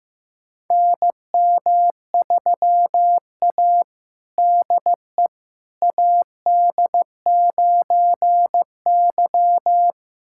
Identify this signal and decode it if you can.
Morse code, heard as a keyed tone, about 15 wpm, 700 Hz. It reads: NM3A DE AD9Y
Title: USA Callsign DE Callsign - 4 Characters - 15wpm